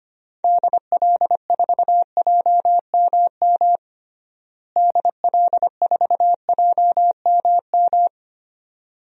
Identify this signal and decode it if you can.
Morse code, heard as a keyed tone, about 25 wpm, 700 Hz. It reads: DL4JMM DL4JMM
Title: DXCC-100 Call Signs - 6 Characters - Repeated 2x - 25wpm